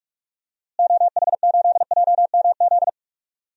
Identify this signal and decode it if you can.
Morse code, heard as a keyed tone, about 48 wpm, 700 Hz. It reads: OH8JMZ
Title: DXCC-100 Call Signs - 6 Characters - Repeated 2x - 45wpm